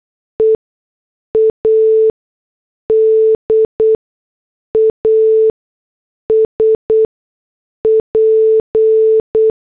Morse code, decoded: EAD AS P